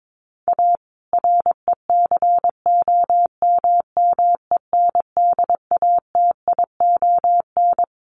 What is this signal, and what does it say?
Morse code, heard as a keyed tone, about 22 wpm, 700 Hz. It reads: A RECOMMENDATION
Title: Sentences from Top 1400 Words 22wpm